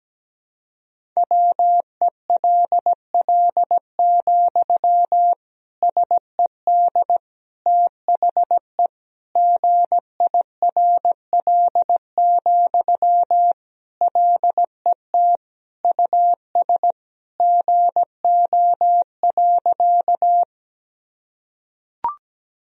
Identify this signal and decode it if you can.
Morse code, heard as a keyed tone, about 17 wpm, 700 Hz. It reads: WELL, SED THE GIRL, LET US GO.
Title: Sentences from Top 400 Words and Abbreviations 17wpm